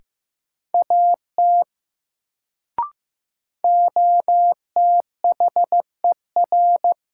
Morse code, heard as a keyed tone, about 15 wpm, 700 Hz. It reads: AT OTHER
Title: Sentences from Top 100 Words - New Word Drill - 15wpm